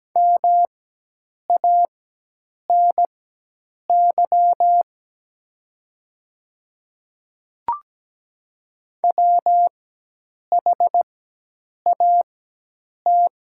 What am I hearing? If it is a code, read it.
Morse code, heard as a keyed tone, about 17 wpm, spaced as for 8 wpm, 700 Hz. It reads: MANY WHAT
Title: 4x Character Spacing - Top 100 Words - 17wpm